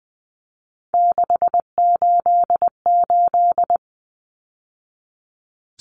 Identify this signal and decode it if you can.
Morse code, heard as a keyed tone, about 20 wpm, 700 Hz. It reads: 688
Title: Numbers - 3-Digit 20wpm